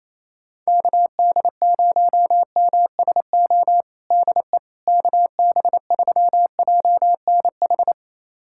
Morse code, heard as a keyed tone, about 28 wpm, 700 Hz. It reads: KD0MSO DE KB3JNH
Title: USA Callsign DE Callsign - 6 Characters - 28wpm